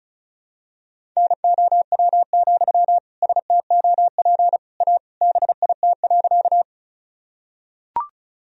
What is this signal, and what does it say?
Morse code, heard as a keyed tone, about 34 wpm, 700 Hz. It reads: NOW, STOP A BIT.
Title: Sentences from Top 1000 Words - Review Part 1 - 35wpm